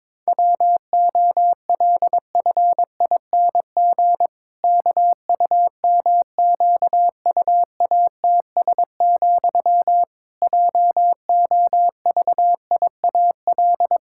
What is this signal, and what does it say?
Morse code, heard as a keyed tone, about 22 wpm, 700 Hz. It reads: WOLFING KUMQUATS, JOVIAL
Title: Pangram Challenge v2 - 22wpm